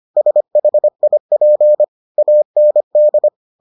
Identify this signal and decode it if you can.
Morse code, heard as a keyed tone, about 25 wpm, 600 Hz. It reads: SHIP AND